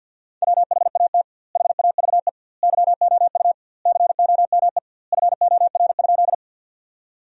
Morse code, heard as a keyed tone, about 50 wpm, 700 Hz. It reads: WHAT HAVE YOU COME FOR?
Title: 2x Word-Spacing - Sentences from Top 100 Words - 50wpm